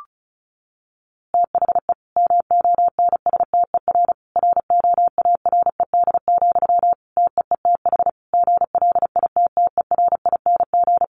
Morse code, heard as a keyed tone, about 34 wpm, 700 Hz. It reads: THE MONSTER ROARED, TEETH GLITTERING
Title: A Sound Of Thunder 35wpm